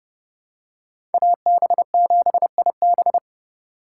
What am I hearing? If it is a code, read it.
Morse code, heard as a keyed tone, about 28 wpm, 700 Hz. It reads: AB7IB